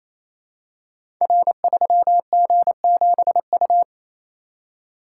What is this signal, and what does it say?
Morse code, heard as a keyed tone, about 27 wpm, 700 Hz. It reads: R3G7U
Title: Code Groups - 5 Letter:Numbers - 28wpm